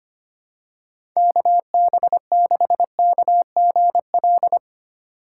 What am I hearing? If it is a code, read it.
Morse code, heard as a keyed tone, about 25 wpm, 700 Hz. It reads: KB6KGL
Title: USA Call Signs - 6 Characters - 4x Character Spacing - 25wpm